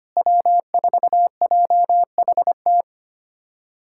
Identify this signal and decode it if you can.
Morse code, heard as a keyed tone, about 25 wpm, 700 Hz. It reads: W4JHT
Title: USA Callsign DE Callsign 25wpm